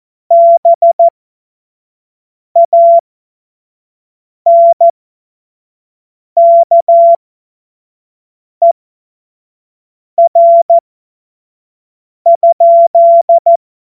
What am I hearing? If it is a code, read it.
Morse code, heard as a keyed tone, about 14 wpm, spaced as for 5 wpm, 680 Hz. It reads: BANKER?
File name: Morsekurs / Witze / Witz 04. April 2023 25 F70